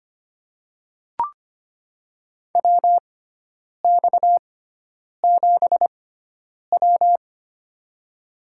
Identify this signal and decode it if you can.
Morse code, heard as a keyed tone, about 25 wpm, spaced as for 9 wpm, 700 Hz. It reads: WX7W
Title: USA Call Signs - 6x Character Spacing - 25wpm